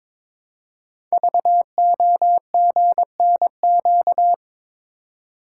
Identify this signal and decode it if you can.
Morse code, heard as a keyed tone, about 22 wpm, 700 Hz. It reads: VOGNQ
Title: Code Groups - 5 Letters - 22wpm